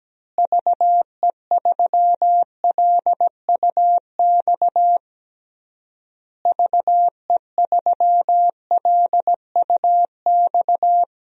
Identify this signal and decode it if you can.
Morse code, heard as a keyed tone, about 17 wpm, 700 Hz. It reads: VE3LUX VE3LUX
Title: DXCC-100 Call Signs - 6 Characters - Repeated 3x - 17wpm